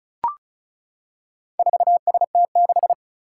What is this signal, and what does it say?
Morse code, heard as a keyed tone, about 36 wpm, 700 Hz. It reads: 4ST6
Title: Four Challenging Letter-Number Combinations 35wpm